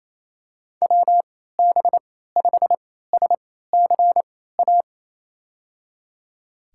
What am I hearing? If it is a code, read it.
Morse code, heard as a keyed tone, about 28 wpm, spaced as for 16 wpm, 700 Hz. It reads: WB5SCA
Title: USA Call Signs - 3x Character Spacing - 28wpm